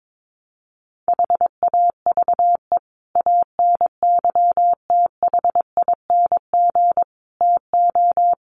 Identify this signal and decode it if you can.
Morse code, heard as a keyed tone, about 22 wpm, 700 Hz. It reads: HAVE ANYTHING TO
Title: Sets of 5 Words 22wpm